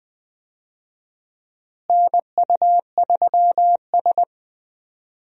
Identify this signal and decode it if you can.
Morse code, heard as a keyed tone, about 20 wpm, 700 Hz. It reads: NU3S
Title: USA Call Signs - 4 Characters - Repeated 3x - 20wpm